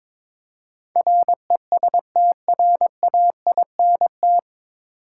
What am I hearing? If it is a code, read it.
Morse code, heard as a keyed tone, about 22 wpm, 700 Hz. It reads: RESTRAINT
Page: Morse Code Ninja practice 22wpm